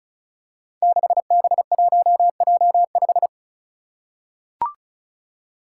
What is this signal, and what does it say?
Morse code, heard as a keyed tone, about 36 wpm, 700 Hz. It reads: 6B1J5
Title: Five Challenging Letter-Number Combinations 35wpm